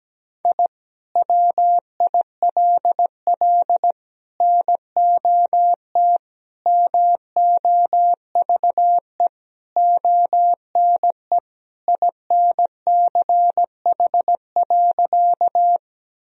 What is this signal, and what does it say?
Morse code, heard as a keyed tone, about 17 wpm, 700 Hz. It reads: I WILL NOT MOVE ONE INCH.
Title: Sentences from Top 1000 Words - Review Part 3 - 17wpm